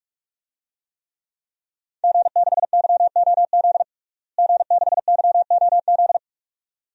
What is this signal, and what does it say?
Morse code, heard as a keyed tone, about 41 wpm, 700 Hz. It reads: G6YOZ G6YOZ
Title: DXCC-100 Call Signs - 5 Characters - Repeated 2x - 45wpm